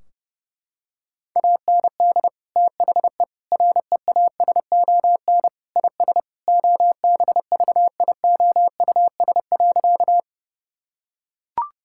Morse code, heard as a keyed tone, about 30 wpm, 700 Hz. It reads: AND THE REASON IS OBVIOUS.
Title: Sentences from Top 1600 Words 30wpm